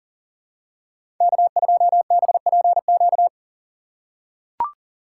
Morse code, heard as a keyed tone, about 40 wpm, 700 Hz. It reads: K2BPQ
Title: USA Call Signs - 5 Characters - 2x Character Spacing - 40wpm